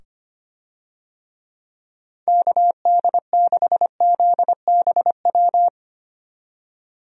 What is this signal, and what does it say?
Morse code, heard as a keyed tone, about 25 wpm, 700 Hz. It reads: KD6ZBW
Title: USA Call Signs - 6 Characters - 25wpm